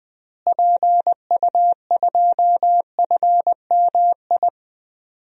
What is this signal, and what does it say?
Morse code, heard as a keyed tone, about 20 wpm, 700 Hz. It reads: PU2FMI
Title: DXCC-100 Call Signs - 6 Characters - Repeated 3x - 20wpm